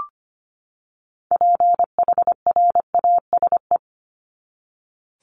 Morse code, heard as a keyed tone, about 25 wpm, 700 Hz. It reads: PHRASE
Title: Top 1501-1750 Words 25wpm